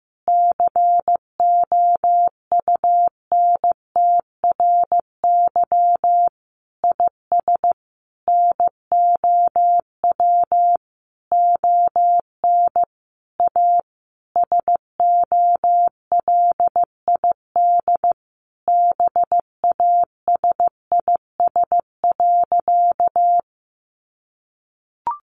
Morse code, heard as a keyed tone, about 15 wpm, 700 Hz. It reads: COUNTRY IS NOW ON A SOLID BASIS.
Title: Sentences from Top 1500 Words 15wpm